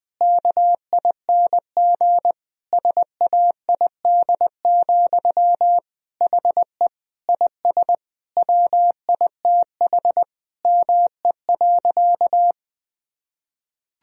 Morse code, heard as a keyed tone, about 20 wpm, 700 Hz. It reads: KING SAID, HE IS WITH ME.
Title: Sentences from Top 1000 Words - Review Part 3 - 20wpm